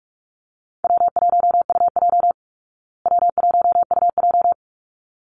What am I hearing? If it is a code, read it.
Morse code, heard as a keyed tone, about 44 wpm, 700 Hz. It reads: W1UJ W1UJ
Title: Call Signs Repeated 3x - 45wpm